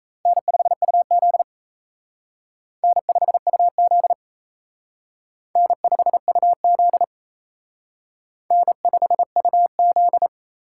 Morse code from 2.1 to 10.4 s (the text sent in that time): N5UZ N5UZ N5UZ